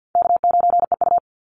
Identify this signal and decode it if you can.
Morse code, heard as a keyed tone, about 50 wpm, 700 Hz. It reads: K9EU
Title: Easy Call Signs - 50wpm